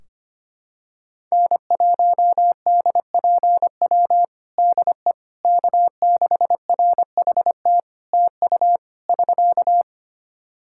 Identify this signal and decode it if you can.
Morse code, heard as a keyed tone, about 25 wpm, 700 Hz. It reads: N1DPW DE K6RHT TU <SK>
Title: Practice QSO - Exchange 4 - Ending - 25wpm